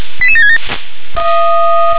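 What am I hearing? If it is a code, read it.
Five-tone selective call (ZVEI), heard as quick runs of five tones, tones of about 70 ms. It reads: 88767
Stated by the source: ZVEI five-tone selective call 88766 received by SDR.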